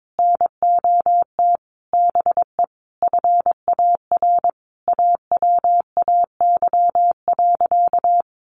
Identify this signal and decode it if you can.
Morse code, heard as a keyed tone, about 22 wpm, 700 Hz. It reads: NOT BE FAR AWAY.
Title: Sentences from Top 1100 Words 22wpm